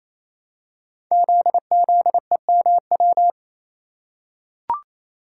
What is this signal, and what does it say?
Morse code, heard as a keyed tone, about 28 wpm, 700 Hz. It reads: ZZEMW
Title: Code Groups - 5 Letters - 28wpm